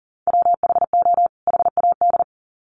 Morse code, heard as a keyed tone, about 40 wpm, 700 Hz. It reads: WHO HAD
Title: Sets of 2 Words 40wpm